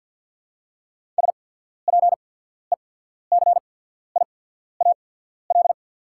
Morse code, heard as a keyed tone, about 51 wpm, spaced as for 14 wpm, 690 Hz. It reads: SPECIAL
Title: 8x Character Spacing - Top 500 Words - 50wpm